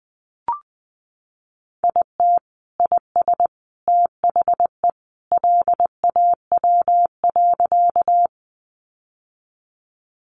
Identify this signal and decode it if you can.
Morse code, heard as a keyed tone, about 20 wpm, 700 Hz. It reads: IT IS THE LAW.
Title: Sentences from Top 600 Words 20wpm